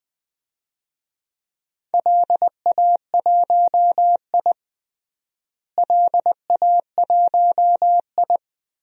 Morse code, heard as a keyed tone, about 21 wpm, 700 Hz. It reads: LA1I LA1I
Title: DXCC-100 Call Signs - 4 Characters - Repeated 2x - 20wpm